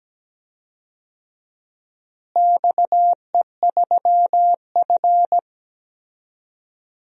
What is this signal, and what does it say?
Morse code, heard as a keyed tone, about 17 wpm, 700 Hz. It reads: XE3F
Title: DXCC-100 Call Signs - 4 Characters - Repeated 2x - 17wpm